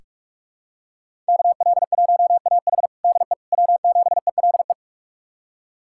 Morse code, heard as a keyed tone, about 42 wpm, 700 Hz. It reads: KL1AH DE W7ELE
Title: USA Callsign DE Callsign 45wpm